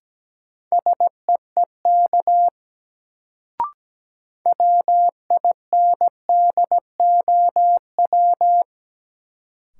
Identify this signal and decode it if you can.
Morse code, heard as a keyed tone, about 17 wpm, 700 Hz. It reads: SEEK WINDOW
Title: Sentences from Top 1600 Words - New Word Drill Alt - 17wpm